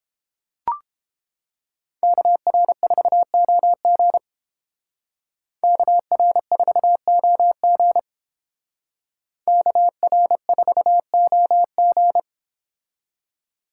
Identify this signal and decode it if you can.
Morse code, heard as a keyed tone, about 29 wpm, 700 Hz. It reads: KR4OG KR4OG KR4OG